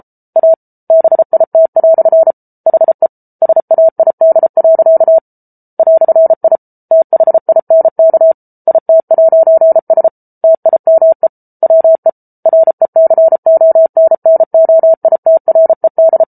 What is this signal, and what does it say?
Morse code, heard as a keyed tone, about 34 wpm, 650 Hz. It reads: A BIT" HE SAID. "I THINK IT'S TIME WE RECONNOITRED